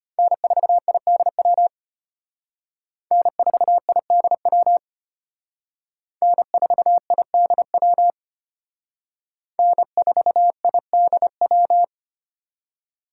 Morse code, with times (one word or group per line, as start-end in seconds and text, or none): 0.2-1.7 N4IDW
3.1-4.8 N4IDW
6.2-8.1 N4IDW
9.6-11.8 N4IDW